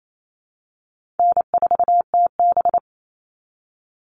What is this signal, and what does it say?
Morse code, heard as a keyed tone, about 28 wpm, 700 Hz. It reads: N4TB
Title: Easy Call Signs Repeated 3x - 28wpm